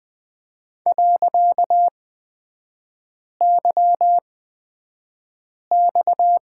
Morse code, heard as a keyed tone, about 20 wpm, spaced as for 5 wpm, 700 Hz. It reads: .YX
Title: 255 Lesson 41e - Period - Three Characters 20wpm